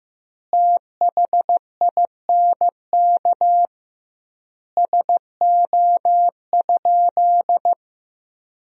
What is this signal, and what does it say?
Morse code, heard as a keyed tone, about 15 wpm, 700 Hz. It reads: THINK SO?
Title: 2x Word-Spacing - Sentences from Top 200 Words - 15wpm